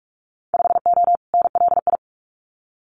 Morse code, heard as a keyed tone, about 44 wpm, 700 Hz. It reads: 5O NLI
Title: Winter Field Day Exchanges 45wpm